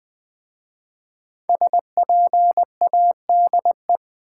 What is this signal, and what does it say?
Morse code, heard as a keyed tone, about 20 wpm, 700 Hz. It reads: SPADE
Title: Easy Words with Hints - 20wpm